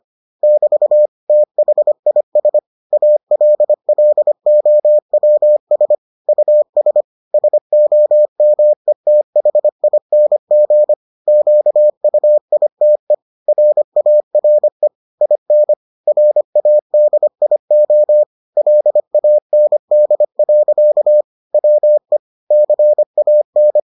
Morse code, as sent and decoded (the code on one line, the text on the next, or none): -...- / - .... .. ... / .- .-.. .-.. --- .-- ... / ..- ... / ... --- -- . - .... .. -. --. / --.- ..- .. - . / .-. .- .-. / .. -. / .-. .- -.. .. --- / .-.. .- -. -.. .-.-.- / .-- . / -.-. .- -.
<BT> THIS ALLOWS US SOMETHING QUITE RARE IN RADIO LAND. WE CAN